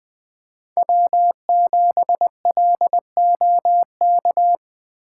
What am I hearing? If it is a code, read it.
Morse code, heard as a keyed tone, about 20 wpm, 700 Hz. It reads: W7LOK